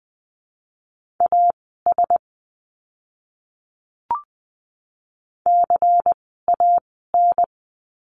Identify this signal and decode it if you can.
Morse code, heard as a keyed tone, about 20 wpm, spaced as for 14 wpm, 700 Hz. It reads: AS CAN
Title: 101 Lesson 17f - Letter u - Words 20wpm